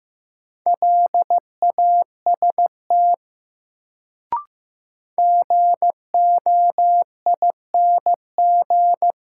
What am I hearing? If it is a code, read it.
Morse code, heard as a keyed tone, about 15 wpm, 700 Hz. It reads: LAST GOING
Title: Sets of 2 Words - New Word Drill - 15wpm